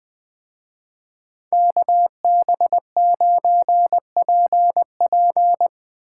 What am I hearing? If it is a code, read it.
Morse code, heard as a keyed tone, about 20 wpm, 700 Hz. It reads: KB9PP